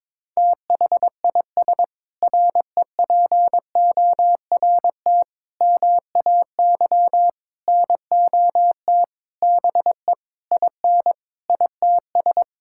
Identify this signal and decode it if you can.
Morse code, heard as a keyed tone, about 21 wpm, 700 Hz. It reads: THIS REPORT MAY NOT BE IN ITS
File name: Sentences with 7-Word Phrases - 22wpm